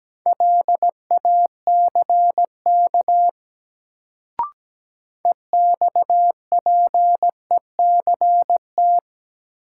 Morse code, heard as a keyed tone, about 17 wpm, 700 Hz. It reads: LACK EXPECT
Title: Sentences from Top 1100 Words - New Word Drill - 17wpm